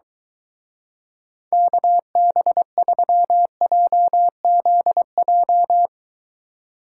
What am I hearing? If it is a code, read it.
Morse code, heard as a keyed tone, about 23 wpm, 700 Hz. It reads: KB3JZJ